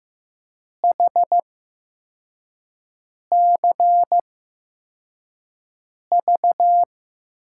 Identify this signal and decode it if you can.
Morse code, heard as a keyed tone, about 15 wpm, spaced as for 4 wpm, 700 Hz. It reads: HCV
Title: USA Call Signs - 5 Characters - 8x Character Spacing - 15wpm